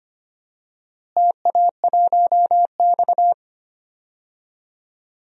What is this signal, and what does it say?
Morse code, heard as a keyed tone, about 25 wpm, 700 Hz. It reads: TA1X
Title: DXCC-100 Call Signs - 4 Characters - Repeated 2x - 25wpm